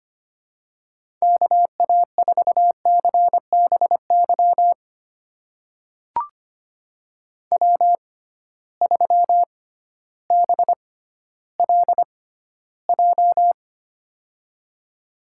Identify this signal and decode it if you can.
Morse code, heard as a keyed tone, about 25 wpm, spaced as for 13 wpm, 700 Hz. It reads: KA4CBY W3BLJ
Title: USA Call Signs - 6x Character Spacing - 25wpm